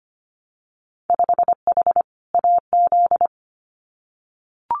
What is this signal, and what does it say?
Morse code, heard as a keyed tone, about 25 wpm, 700 Hz. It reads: 5H AZ
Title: Winter Field Day Exchanges Encore 25wpm